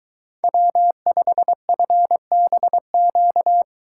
Morse code, heard as a keyed tone, about 22 wpm, 700 Hz. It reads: W5FBQ